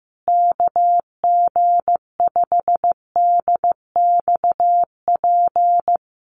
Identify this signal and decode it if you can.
Morse code, heard as a keyed tone, about 15 wpm, 700 Hz. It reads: KG5DXP
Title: USA Call Signs Repeated 3x 15wpm